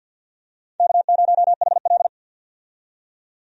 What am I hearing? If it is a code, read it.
Morse code, heard as a keyed tone, about 52 wpm, 700 Hz. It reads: K0HL
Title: USA Call Signs - 4 Characters - 50wpm